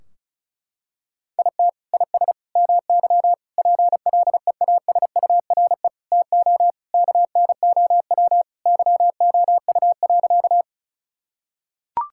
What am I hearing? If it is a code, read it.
Morse code, heard as a keyed tone, about 36 wpm, 700 Hz. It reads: IT IS MY PLEASURE TO KNOW YOU.
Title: Sentences from Top 1600 Words 35wpm